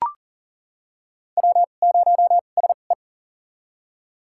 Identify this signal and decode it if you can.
Morse code, heard as a keyed tone, about 40 wpm, spaced as for 29 wpm, 700 Hz. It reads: W0SE
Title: USA Call Signs - 4 Characters - 2x Character Spacing - 40wpm